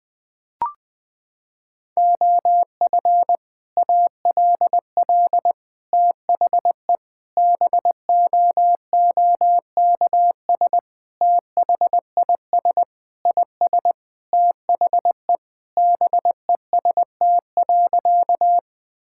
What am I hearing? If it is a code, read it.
Morse code, heard as a keyed tone, about 21 wpm, 700 Hz. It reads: OF ALL THE BOOKS THIS IS THE BEST.
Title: Sentences with 3-Word Phrases - 20wpm